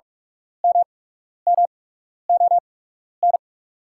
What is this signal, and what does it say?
Morse code, heard as a keyed tone, about 45 wpm, spaced as for 12 wpm, 690 Hz. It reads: MMON
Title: 8x Character Spacing - Top 500 Words - 45wpm